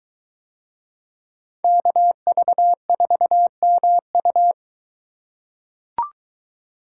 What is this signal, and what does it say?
Morse code, heard as a keyed tone, about 21 wpm, 700 Hz. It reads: KV4MU